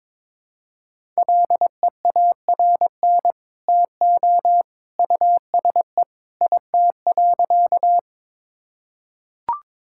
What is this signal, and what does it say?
Morse code, heard as a keyed tone, about 22 wpm, 700 Hz. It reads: LEARN TO USE IT.
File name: practice / 22wpm / Sentences with 2-Word Phrases - 22wpm